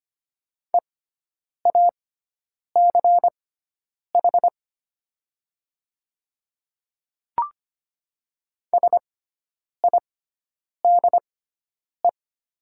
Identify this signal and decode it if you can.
Morse code, heard as a keyed tone, about 25 wpm, spaced as for 9 wpm, 700 Hz. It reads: EACH SIDE
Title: 6x Character Spacing - Top 200 Words - 25wpm